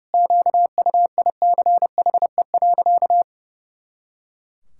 Morse code, heard as a keyed tone, about 30 wpm, 700 Hz. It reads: QUICHE.
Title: Pangram Challenge v2 - 30wpm